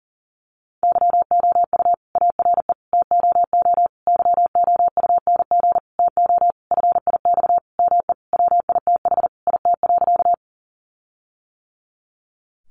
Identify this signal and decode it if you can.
Morse code, heard as a keyed tone, about 40 wpm, 700 Hz. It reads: YOU ARE TOO YOUNG TO FIX ME WITH IT.
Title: Sentences from Top 1400 Words 40wpm